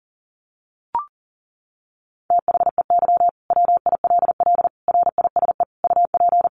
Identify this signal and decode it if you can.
Morse code, heard as a keyed tone, about 40 wpm, 700 Hz. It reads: THEY WILL RISE UP
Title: Sentences from Top 900 Words 40wpm